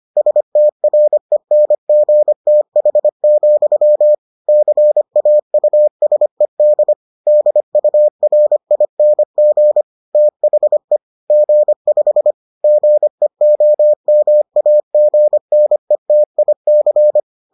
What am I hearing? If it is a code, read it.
Morse code, heard as a keyed tone, about 25 wpm, 600 Hz. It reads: STRENGTH, CAUSED DURING THE G5 GEOMAGNETIC